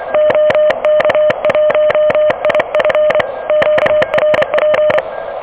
Morse code, heard as a keyed tone, about 25 wpm, 600 Hz. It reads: OK1IF QRP
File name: OK QRP 2023 / OK QRP 2023